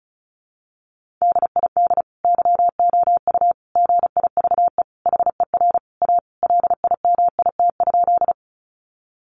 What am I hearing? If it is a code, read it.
Morse code, heard as a keyed tone, about 34 wpm, 700 Hz. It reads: DID YOU GIVE HER A LIMIT?
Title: Sentences from Top 1500 Words 35wpm